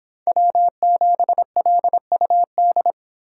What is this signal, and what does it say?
Morse code, heard as a keyed tone, about 25 wpm, 700 Hz. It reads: W7LUD